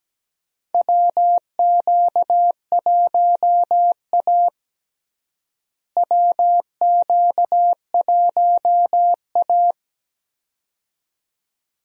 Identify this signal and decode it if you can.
Morse code, heard as a keyed tone, about 17 wpm, 700 Hz. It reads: WQ1A WQ1A
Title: USA Call Signs - 4 Characters - Repeated 2x - 17wpm